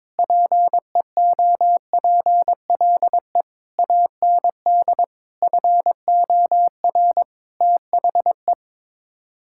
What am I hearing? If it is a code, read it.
Morse code, heard as a keyed tone, about 22 wpm, 700 Hz. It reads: PEOPLE AND FOR THE
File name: Sets of 9 Words - 22wpm